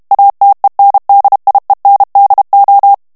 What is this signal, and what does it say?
Morse code, heard as a keyed tone, about 32 wpm, 800 Hz. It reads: ATENDIENDO
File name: morse mp3 / ESPAGNOL TEXTE EN CLAIR 2 VITESSE 30 MOTS MINUTES